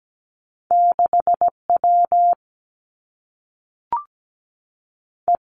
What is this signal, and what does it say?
Morse code, heard as a keyed tone, about 17 wpm, 700 Hz. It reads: 6 W E
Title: Two Challenging Letter-Number Combinations 17wpm